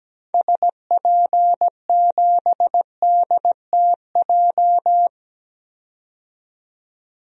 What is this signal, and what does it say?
Morse code, heard as a keyed tone, about 17 wpm, 700 Hz. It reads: SP7DTJ
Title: DXCC-100 Call Signs - 6 Characters - 17wpm